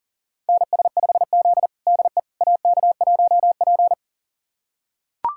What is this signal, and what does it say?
Morse code, heard as a keyed tone, about 40 wpm, 700 Hz. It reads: NS5Z DE AK1P
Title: USA Callsign DE Callsign - 4 Characters - 40wpm